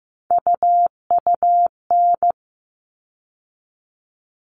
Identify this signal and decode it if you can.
Morse code, heard as a keyed tone, about 15 wpm, 700 Hz. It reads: UUN
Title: All Cut Numbers - 3-Digit 15wpm